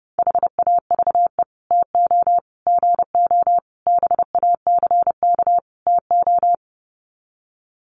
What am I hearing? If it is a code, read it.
Morse code, heard as a keyed tone, about 30 wpm, 700 Hz. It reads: HAVE TO GO BACK TO
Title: Sets of 5 Words Encore2 30wpm